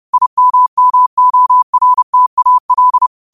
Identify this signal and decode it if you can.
Morse code, heard as a keyed tone, about 30 wpm, 1000 Hz. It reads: IMMORTAL
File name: 30wpm